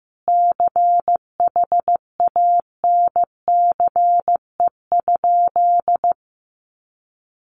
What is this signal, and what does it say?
Morse code, heard as a keyed tone, about 15 wpm, 700 Hz. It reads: CHANCE?
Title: Sentences from Top 800 Words 15wpm